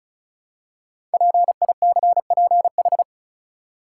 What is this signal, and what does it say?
Morse code, heard as a keyed tone, about 36 wpm, 690 Hz. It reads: PICPH
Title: Code Groups - 5 Letters - 35wpm